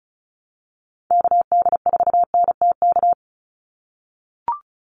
Morse code, heard as a keyed tone, about 34 wpm, 700 Hz. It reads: KD4NTK
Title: USA Call Signs 35wpm